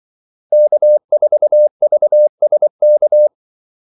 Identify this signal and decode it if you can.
Morse code, heard as a keyed tone, about 24 wpm, 600 Hz. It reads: K4VSK
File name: Callsigns-24wpm